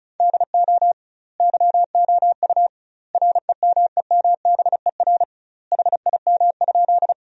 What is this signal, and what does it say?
Morse code, heard as a keyed tone, about 34 wpm, 700 Hz. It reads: DO YOU REMEMBER HIM?
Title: 2x Word-Spacing - Sentences from Top 400 Words - 35wpm